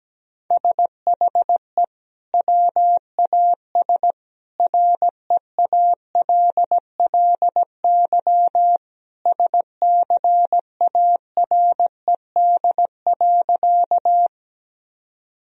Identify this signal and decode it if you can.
Morse code, heard as a keyed tone, about 17 wpm, 700 Hz. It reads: SHE WAS REALLY SCARED.
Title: Sentences from Top 1000 Words - Review Part 4 - 17wpm